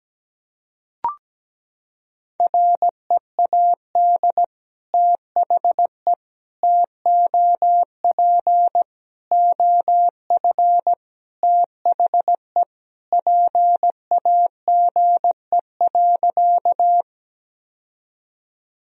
Morse code, heard as a keyed tone, about 17 wpm, 700 Hz. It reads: READ THE TOP OF THE PAGE.
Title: Sentences from Top 1000 Words - Review Part 2 - 17wpm